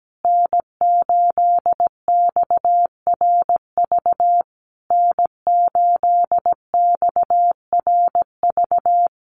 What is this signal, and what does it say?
Morse code, heard as a keyed tone, about 17 wpm, 700 Hz. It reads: N8XRV N8XRV
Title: USA Call Signs Repeated 3x 17wpm